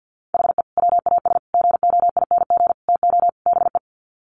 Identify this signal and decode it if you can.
Morse code, heard as a keyed tone, about 50 wpm, 700 Hz. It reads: HE WAS GOING TO BE
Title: Sets of 5 Words 50wpm